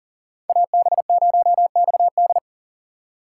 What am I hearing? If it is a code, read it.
Morse code, heard as a keyed tone, about 39 wpm, 690 Hz. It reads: AB0XD
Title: USA Call Signs - 4x Character Spacing - 40wpm